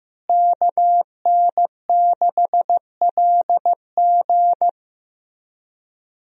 Morse code, heard as a keyed tone, about 15 wpm, 700 Hz. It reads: KN6LG
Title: USA Call Signs - 5 Characters - Repeated 2x - 15wpm